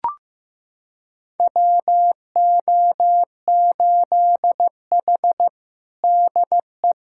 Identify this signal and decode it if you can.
Morse code, heard as a keyed tone, about 16 wpm, 700 Hz. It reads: WO8H DE
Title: USA Callsign DE Callsign - 4 Characters - 15wpm